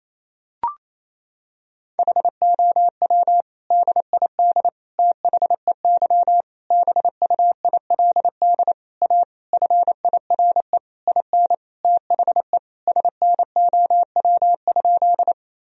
Morse code, heard as a keyed tone, about 27 wpm, 700 Hz. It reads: HOW DID THEY BUILD A FIRE IN THE SNOW?
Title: Sentences from Top 1000 Words - Review Part 3 - 28wpm